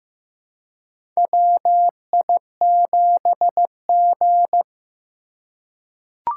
Morse code, heard as a keyed tone, about 15 wpm, 700 Hz. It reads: WI7G
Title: USA Call Signs - 4 Characters - 3x Character Spacing - 15wpm